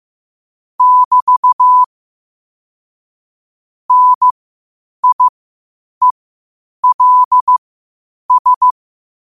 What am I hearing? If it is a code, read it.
Morse code, heard as a keyed tone, about 15 wpm, spaced as for 8 wpm, 1000 Hz. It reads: <BT> NIELS
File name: qotdfiles / 8wpm